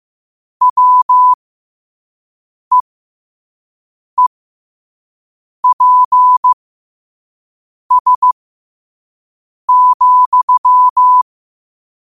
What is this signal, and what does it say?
Morse code, heard as a keyed tone, about 15 wpm, spaced as for 5 wpm, 1000 Hz. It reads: WEEPS,
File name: 2024-10-07-5wpm